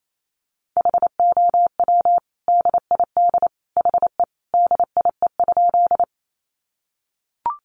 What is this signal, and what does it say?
Morse code, heard as a keyed tone, about 28 wpm, 700 Hz. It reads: HOW DID HE DIE?
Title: Sentences from Top 1300 Words 28wpm